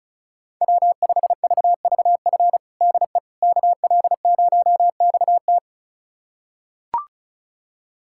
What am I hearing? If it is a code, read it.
Morse code, heard as a keyed tone, about 36 wpm, 700 Hz. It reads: W5VVF DE KL0XT
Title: USA Callsign DE Callsign - 5 Characters - 35wpm